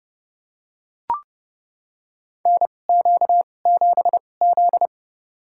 Morse code, heard as a keyed tone, about 30 wpm, spaced as for 22 wpm, 700 Hz. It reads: NQ7Z
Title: USA Call Signs - 2x Character Spacing - 30wpm